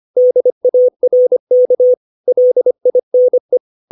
Morse code, heard as a keyed tone, about 25 wpm, 500 Hz. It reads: DARK LINE
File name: 632 words / 632 Words 25WPM